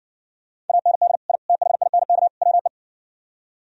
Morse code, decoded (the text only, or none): AND I THINK WE